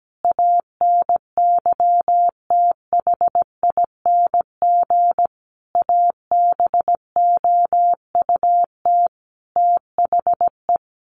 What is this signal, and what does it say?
Morse code, heard as a keyed tone, about 17 wpm, 700 Hz. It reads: ANYTHING ABOUT THE
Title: Sentences from Top 1000 Words 17wpm